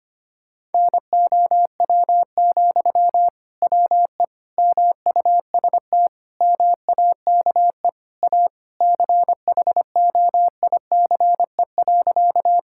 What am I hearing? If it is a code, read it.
Morse code, heard as a keyed tone, about 25 wpm, 700 Hz. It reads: NOW, WE MUST MAKE A CHOICE.
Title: Sentences from Top 1000 Words - Review Part 6 - 25wpm